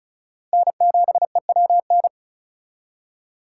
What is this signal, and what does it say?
Morse code, heard as a keyed tone, about 44 wpm, 700 Hz. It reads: N7EWN